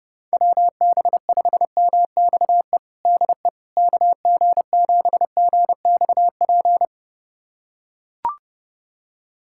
Morse code, heard as a keyed tone, about 32 wpm, 700 Hz. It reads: WB5MXE DE KG7GXP
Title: USA Callsign DE Callsign - 6 Characters - 30wpm